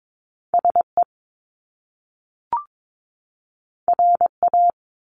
Morse code, heard as a keyed tone, about 22 wpm, 700 Hz. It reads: SE RA
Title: Top Two Letter Combinations in Words 22wpm